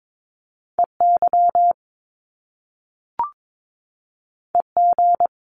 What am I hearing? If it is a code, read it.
Morse code, heard as a keyed tone, about 22 wpm, 700 Hz. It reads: EY EG
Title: Top Two Letter Combinations in Words 22wpm